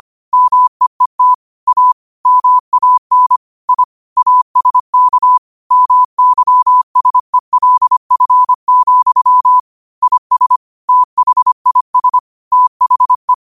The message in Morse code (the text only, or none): MEET A MAN I ASK MYSELF, IS THIS THE